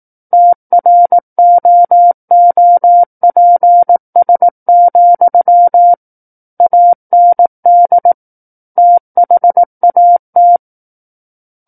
Morse code, heard as a keyed tone, about 18 wpm, 700 Hz. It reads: TROOPS, AND THAT
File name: 18 WPM